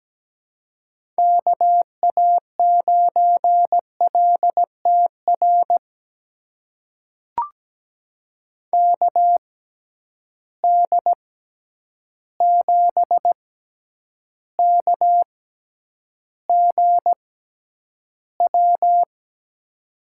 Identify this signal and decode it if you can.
Morse code, heard as a keyed tone, about 17 wpm, spaced as for 9 wpm, 700 Hz. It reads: KA9LTR KD7KGW
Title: USA Call Signs - 6 Characters - 6x Character Spacing - 17wpm